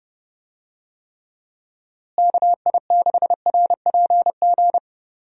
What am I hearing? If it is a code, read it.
Morse code, heard as a keyed tone, about 30 wpm, 700 Hz. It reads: KI6RPG
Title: USA Call Signs - 6 Characters - Repeated 2x - 30wpm